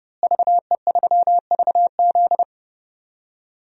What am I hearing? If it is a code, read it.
Morse code, heard as a keyed tone, about 30 wpm, 700 Hz. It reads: VE3VZ
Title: DXCC-100 Call Signs - 5 Characters - Repeated 2x - 30wpm